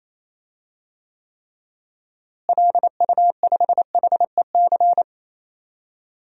Morse code, heard as a keyed tone, about 28 wpm, 700 Hz. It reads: LU5HEC
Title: DXCC-100 Call Signs - 6 Characters - Repeated 2x - 28wpm